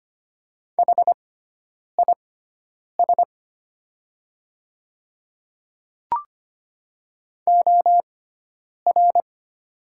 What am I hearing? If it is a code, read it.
Morse code, heard as a keyed tone, about 25 wpm, spaced as for 9 wpm, 700 Hz. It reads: HIS OR
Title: 6x Character Spacing - Top 100 Words - 25wpm